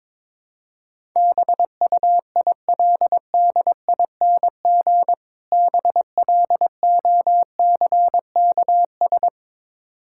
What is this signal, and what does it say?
Morse code, heard as a keyed tone, about 22 wpm, 700 Hz. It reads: BUILDING BLOCKS
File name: Difficult Words with Hints - 22wpm